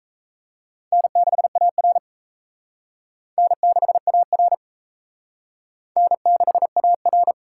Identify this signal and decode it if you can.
Morse code, heard as a keyed tone, about 34 wpm, 700 Hz. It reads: N6AR N6AR N6AR
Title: Speed Racing - USA 1x2 Call Signs 28wpm